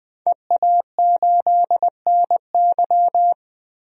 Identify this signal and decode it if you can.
Morse code, heard as a keyed tone, about 20 wpm, 700 Hz. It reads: EA8NY
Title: DXCC-100 Call Signs - 5 Characters - Repeated 2x - 20wpm